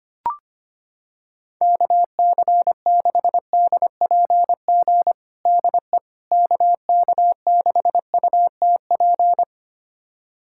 Morse code, heard as a keyed tone, about 25 wpm, 700 Hz. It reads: KC6DPG DE KK6UTP
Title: USA Callsign DE Callsign - 6 Characters - 25wpm